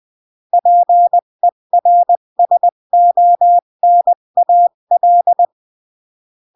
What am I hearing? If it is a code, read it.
Morse code, heard as a keyed tone, about 20 wpm, 700 Hz. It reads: PERSONAL
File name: CW / jodorowsky / 82 commandments 20-15